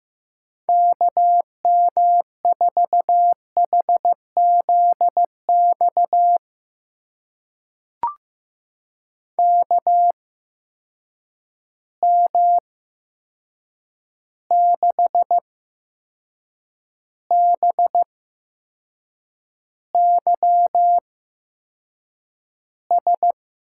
Morse code, heard as a keyed tone, about 15 wpm, spaced as for 6 wpm, 700 Hz. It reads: KM4HZX KM6BYS
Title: USA Call Signs - 6 Characters - 8x Character Spacing - 15wpm